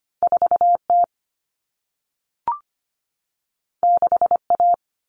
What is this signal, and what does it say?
Morse code, heard as a keyed tone, about 25 wpm, 700 Hz. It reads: 4T 6A
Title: Common Cut Numbers - 2-Digit 25wpm